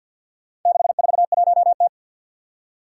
Morse code, heard as a keyed tone, about 47 wpm, 700 Hz. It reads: 641T
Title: Four Challenging Letter-Number Combinations 50wpm